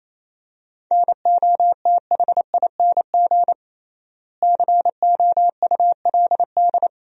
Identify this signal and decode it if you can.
Morse code, heard as a keyed tone, about 28 wpm, 700 Hz. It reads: NOTHING COULD
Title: 3x Word-Spacing - Sentences from Top 500 Words - 28wpm